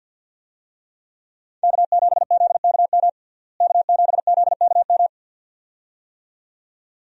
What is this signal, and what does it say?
Morse code, heard as a keyed tone, about 51 wpm, 700 Hz. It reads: K7ZKM K7ZKM
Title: USA Call Signs - 5 Characters - Repeated 2x - 50wpm